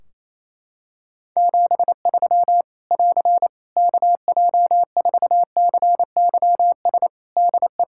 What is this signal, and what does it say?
Morse code, heard as a keyed tone, about 27 wpm, 700 Hz. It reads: 73 <AR> KJ4CYS DE
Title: Practice QSO - Exchange 4 - Ending - 28wpm